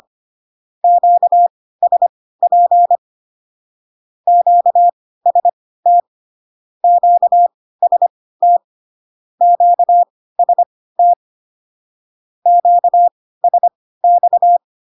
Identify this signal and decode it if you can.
Morse code, heard as a keyed tone, about 25 wpm, spaced as for 16 wpm, 700 Hz. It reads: QSP QST QST QST QSX
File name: cw / Q-codes most common repeated 3 times 10 25wpm ik1hge